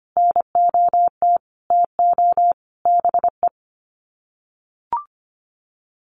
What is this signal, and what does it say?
Morse code, heard as a keyed tone, about 25 wpm, 700 Hz. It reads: NOT TO BE
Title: Sets of 5 Words Encore2 25wpm